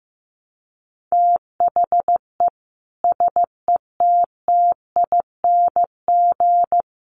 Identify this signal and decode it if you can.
Morse code, heard as a keyed tone, about 15 wpm, 700 Hz. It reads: THE SETTING
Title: Sentences from Top 1300 Words 15wpm